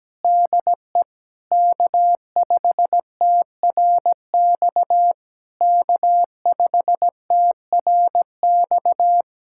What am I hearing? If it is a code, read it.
Morse code, heard as a keyed tone, about 17 wpm, 700 Hz. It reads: DE K5TRX K5TRX
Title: SOTA QSO Practice 17wpm